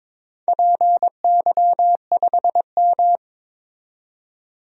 Morse code, heard as a keyed tone, about 22 wpm, 700 Hz. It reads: PY5M
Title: DXCC-100 Call Signs - 4 Characters - 22wpm